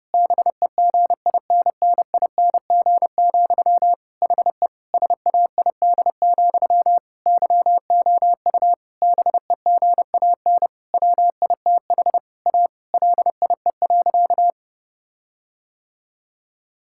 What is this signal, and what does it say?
Morse code, heard as a keyed tone, about 30 wpm, 700 Hz. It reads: BEGINNING, HE SAID, YOU BEGAN WITH A LIE.
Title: Sentences from Top 1600 Words 30wpm